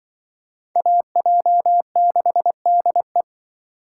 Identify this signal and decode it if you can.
Morse code, heard as a keyed tone, about 23 wpm, 700 Hz. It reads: AJ6DE
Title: Speed Racing - USA Call Signs 20wpm